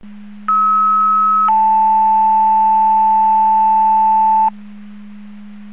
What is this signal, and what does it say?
Two-tone page, heard as a one-second tone, then a three-second tone, about 1290 then 880 Hz. It is heard from 0.5 to 4.5 s.